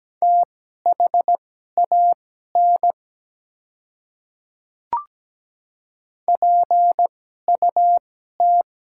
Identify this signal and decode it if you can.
Morse code, heard as a keyed tone, about 17 wpm, spaced as for 12 wpm, 690 Hz. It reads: THAN PUT
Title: 2x Character Spacing - Top 300 Words - 17wpm